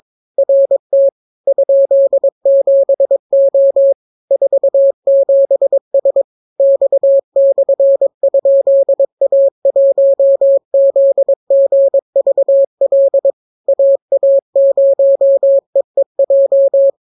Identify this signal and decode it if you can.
Morse code, heard as a keyed tone, about 22 wpm, 550 Hz. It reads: RT ?7O 47S X/?A1ZGVL AA0EEJ